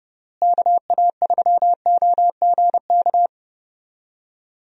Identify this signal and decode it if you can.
Morse code, heard as a keyed tone, about 32 wpm, 700 Hz. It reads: KA3OGK
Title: USA Call Signs - 6 Characters - Repeated 3x - 30wpm